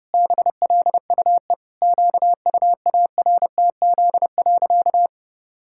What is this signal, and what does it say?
Morse code, heard as a keyed tone, about 30 wpm, 700 Hz. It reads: BLUE QUARTZ.
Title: Pangram Challenge v2 - 30wpm